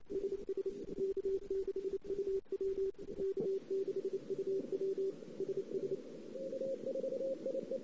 Morse code, heard as a keyed tone, about 32 wpm, 410 Hz. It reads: SS K6UW K6UW SS
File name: CW / 2008-ss-cw-k5tr-op-n5rz -17:00